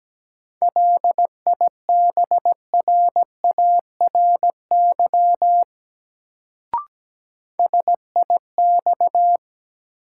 Morse code, heard as a keyed tone, about 17 wpm, 700 Hz. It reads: LIBRARY SIX
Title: Sentences from Top 600 Words - New Word Drill - 17wpm